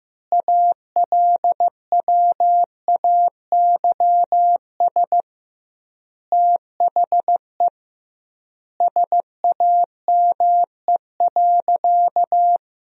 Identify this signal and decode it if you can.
Morse code, heard as a keyed tone, about 16 wpm, 700 Hz. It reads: ALWAYS THE SAME.
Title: 2x Word-Spacing - Sentences from Top 300 Words - 15wpm